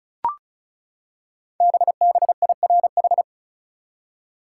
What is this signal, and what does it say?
Morse code, heard as a keyed tone, about 36 wpm, 700 Hz. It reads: BBIRH
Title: Code Groups - 5 Letters - 35wpm